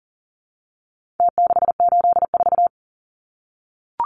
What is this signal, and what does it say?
Morse code, heard as a keyed tone, about 40 wpm, 700 Hz. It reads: T684
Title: Numbers - Contest Style - 4-Digit 40wpm